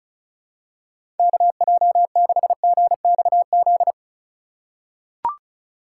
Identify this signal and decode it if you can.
Morse code, heard as a keyed tone, about 34 wpm, 700 Hz. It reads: KJ6GXZ
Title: USA Call Signs - 3x Character Spacing - 35wpm